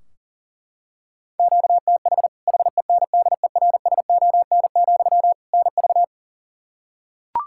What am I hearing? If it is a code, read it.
Morse code, heard as a keyed tone, about 37 wpm, 700 Hz. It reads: QTH HENDERSON, NV
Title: Practice QSO - Exchange 1 - Beginning - 40wpm